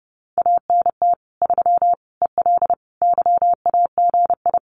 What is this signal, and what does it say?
Morse code, heard as a keyed tone, about 30 wpm, 700 Hz. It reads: ANT 3 EL YAGI
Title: Common QSO Phrases Encore 30wpm